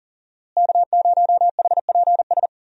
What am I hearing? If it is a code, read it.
Morse code, heard as a keyed tone, about 40 wpm, 700 Hz. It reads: K0HPS
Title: USA Call Signs - 5 Characters - Repeated 3x - 40wpm